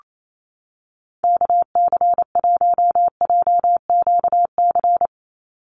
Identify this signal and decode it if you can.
Morse code, heard as a keyed tone, about 28 wpm, 700 Hz. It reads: KC1JQC
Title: USA Call Signs 28wpm